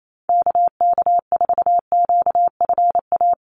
Morse code, heard as a keyed tone, about 28 wpm, 700 Hz. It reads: KK4QFA
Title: USA Call Signs Repeated 3x 28wpm